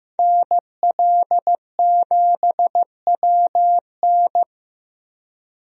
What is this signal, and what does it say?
Morse code, heard as a keyed tone, about 15 wpm, 700 Hz. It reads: NL7WN
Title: USA Call Signs - 5 Characters - 15wpm